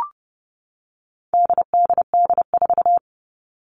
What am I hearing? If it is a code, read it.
Morse code, heard as a keyed tone, about 30 wpm, 700 Hz. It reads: DDD4
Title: All Cut Numbers - 4-Digit 30wpm